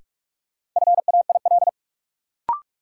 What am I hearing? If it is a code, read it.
Morse code, heard as a keyed tone, about 45 wpm, 700 Hz. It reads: FAIL
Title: Sentences from Top 1100 Words - New Word Drill - 45wpm